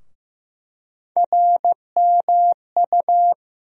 Morse code, heard as a keyed tone, about 15 wpm, 700 Hz. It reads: RMU
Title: USA Call Sign Postfixes - 3 Letters 15wpm